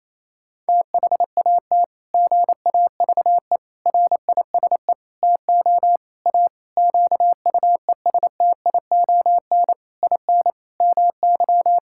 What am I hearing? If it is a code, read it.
Morse code, heard as a keyed tone, about 28 wpm, 700 Hz. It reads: THAT GAVE RISE TO A QUESTION IN MY
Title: Sentences from Top 1000 Words - Review Part 6 - 28wpm